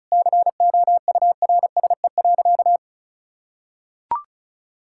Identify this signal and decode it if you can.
Morse code, heard as a keyed tone, about 34 wpm, 700 Hz. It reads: COURSE.
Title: Sentences from Top 1600 Words 35wpm